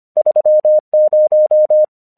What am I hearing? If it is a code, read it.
Morse code, heard as a keyed tone, about 25 wpm, 610 Hz. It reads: 30
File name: 1 English 25wpm / AQC11